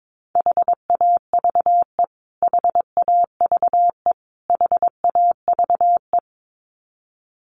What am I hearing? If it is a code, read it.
Morse code, heard as a keyed tone, about 22 wpm, 700 Hz. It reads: HAVE HAVE HAVE
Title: Top 200 Words Repeated 22wpm